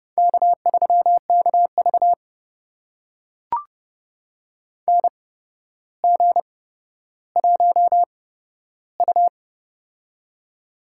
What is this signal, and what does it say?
Morse code, heard as a keyed tone, about 30 wpm, spaced as for 13 wpm, 700 Hz. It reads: K3KV NG1U
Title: USA Call Signs - 4 Characters - 8x Character Spacing - 30wpm